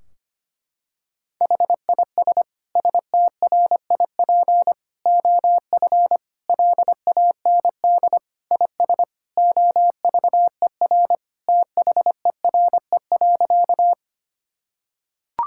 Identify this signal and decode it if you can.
Morse code, heard as a keyed tone, about 25 wpm, 700 Hz. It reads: HIS STRIP OF LAND IS OVER THERE.
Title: Sentences from Top 1600 Words 25wpm